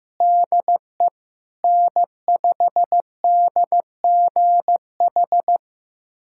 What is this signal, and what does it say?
Morse code, heard as a keyed tone, about 16 wpm, 700 Hz. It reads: DE N5DGH
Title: USA Callsign DE Callsign - 5 Characters - 15wpm